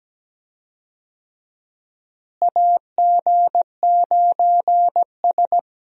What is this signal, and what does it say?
Morse code, heard as a keyed tone, about 17 wpm, 700 Hz. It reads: AG9S
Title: USA Call Signs - 4 Characters - Repeated 2x - 17wpm